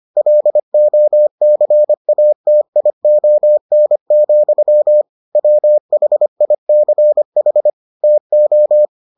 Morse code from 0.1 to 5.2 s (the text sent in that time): LOCATION,